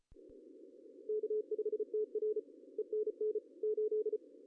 Morse code, heard as a keyed tone, about 34 wpm, 410 Hz. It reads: K5TR ENN 8
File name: CW / cqww-cw-k5tr